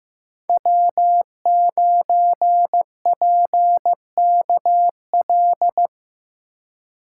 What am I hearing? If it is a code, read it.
Morse code, heard as a keyed tone, about 15 wpm, 700 Hz. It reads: W9PKL
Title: USA Call Signs - 5 Characters - 4x Character Spacing - 15wpm